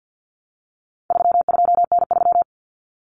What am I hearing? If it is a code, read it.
Morse code, heard as a keyed tone, about 50 wpm, 700 Hz. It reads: 32N3
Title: Common Cut Numbers - 4-Digit 50wpm